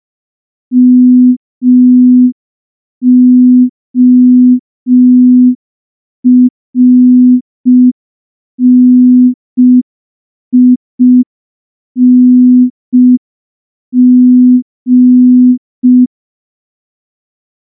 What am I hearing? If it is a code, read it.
Morse code, heard as a keyed tone, about 5 wpm, 250 Hz. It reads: MORNING
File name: morse / morse-training-05WPM